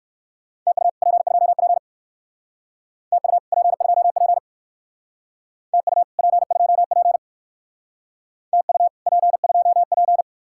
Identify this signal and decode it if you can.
Morse code, heard as a keyed tone, about 54 wpm, 700 Hz. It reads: TU P2P TU P2P TU P2P TU P2P
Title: Speed-Racing Common QSO Phrases - 45wpm